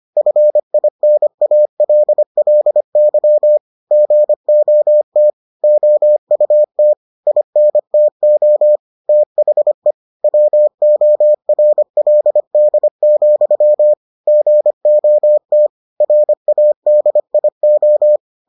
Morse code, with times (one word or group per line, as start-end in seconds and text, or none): 0.2-3.6 FINALLY
3.9-5.3 GOT
5.6-6.9 OUT
7.3-8.8 INTO
9.1-9.9 THE
10.2-13.9 WORLD,
14.3-15.7 GOT
16.0-18.2 RADIO